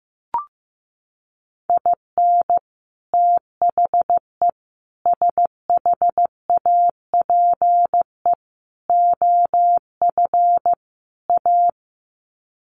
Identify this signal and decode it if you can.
Morse code, heard as a keyed tone, about 15 wpm, 700 Hz. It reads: IN THE SHAPE OF A
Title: Sets of 5 Words Encore 15wpm